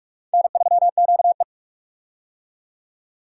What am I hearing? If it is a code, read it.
Morse code, heard as a keyed tone, about 47 wpm, 700 Hz. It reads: N3QE
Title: K1USN SST QSO Practice - 45wpm